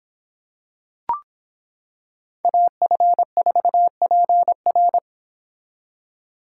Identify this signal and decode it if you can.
Morse code, heard as a keyed tone, about 26 wpm, 700 Hz. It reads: AF4PR